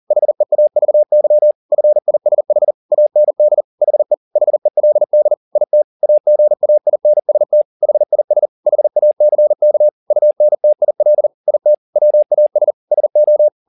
Morse code, heard as a keyed tone, about 40 wpm, 600 Hz. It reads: HEAVY FISH AND HE HELD IT AGAINST HIS HACK UNTIL IT WAS SO